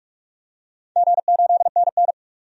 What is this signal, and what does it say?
Morse code, heard as a keyed tone, about 46 wpm, 700 Hz. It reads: G8NN
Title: DXCC-100 Call Signs - 4 Characters - 45wpm